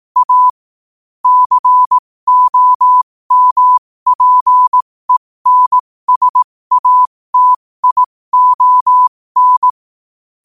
Morse code, heard as a keyed tone, about 18 wpm, spaced as for 15 wpm, 1000 Hz. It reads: A COMPENSATION
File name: qotdfiles / 15wpm